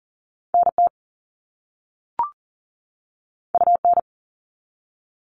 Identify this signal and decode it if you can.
Morse code, heard as a keyed tone, about 41 wpm, 700 Hz. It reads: NT UN
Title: All Cut Numbers - 2-Digit 40wpm